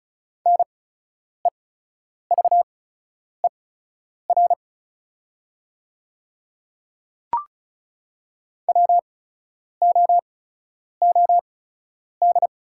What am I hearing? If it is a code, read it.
Morse code, heard as a keyed tone, about 34 wpm, spaced as for 10 wpm, 700 Hz. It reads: NEVER WOOD
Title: 8x Character Spacing - Top 300 Words - 35wpm